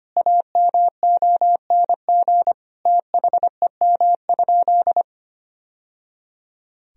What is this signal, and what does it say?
Morse code, heard as a keyed tone, about 25 wpm, 700 Hz. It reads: AMONG THEM?
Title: Sentences from Top 1000 Words - Review Part 3 - 25wpm